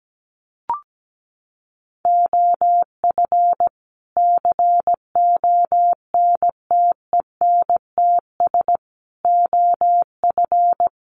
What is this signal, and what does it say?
Morse code, heard as a keyed tone, about 17 wpm, 700 Hz. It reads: OF CONTENTS OF
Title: Sets of 5 Words Encore2 17wpm